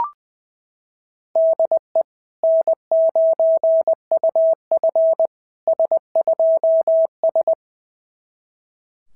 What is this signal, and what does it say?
Morse code, heard as a keyed tone, about 20 wpm, 690 Hz. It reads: DE N9UF S2S
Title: SOTA QSO Practice 20wpm